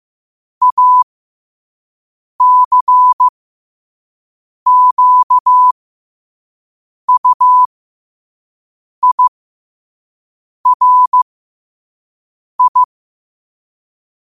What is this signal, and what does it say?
Morse code, heard as a keyed tone, about 15 wpm, spaced as for 5 wpm, 1000 Hz. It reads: ACQUIRI
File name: qotdfiles / 5wpm